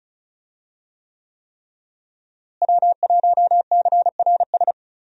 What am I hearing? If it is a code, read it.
Morse code, heard as a keyed tone, about 36 wpm, 700 Hz. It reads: W1CRS
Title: USA Call Signs - 5 Characters - Repeated 3x - 35wpm